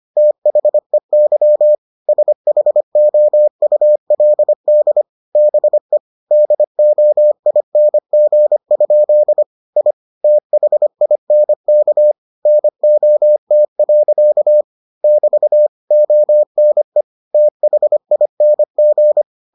Morse code, heard as a keyed tone, about 25 wpm, 600 Hz. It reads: THEY SHOULD BE DOING? I THINK NOT. <BT> ONE THING